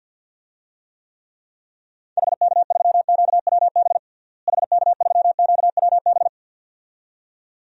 Morse code, heard as a keyed tone, about 47 wpm, 700 Hz. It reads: HK3QWB HK3QWB
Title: DXCC-100 Call Signs - 6 Characters - Repeated 2x - 50wpm